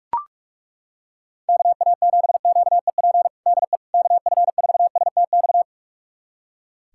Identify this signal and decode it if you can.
Morse code, heard as a keyed tone, about 41 wpm, 700 Hz. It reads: KA7QEP DE KF4STX
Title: USA Callsign DE Callsign 45wpm